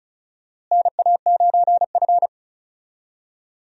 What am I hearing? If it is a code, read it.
Morse code, heard as a keyed tone, about 34 wpm, 700 Hz. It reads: NA9F
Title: USA Call Signs - 4 Characters - Repeated 2x - 35wpm